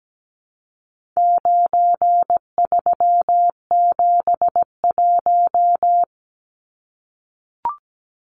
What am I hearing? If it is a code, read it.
Morse code, heard as a keyed tone, about 17 wpm, 700 Hz. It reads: 9371
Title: Numbers - 4-Digit 17wpm